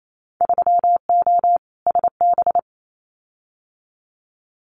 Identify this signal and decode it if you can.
Morse code, heard as a keyed tone, about 28 wpm, 700 Hz. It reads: 3O SB
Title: Winter Field Day Exchanges Encore 28wpm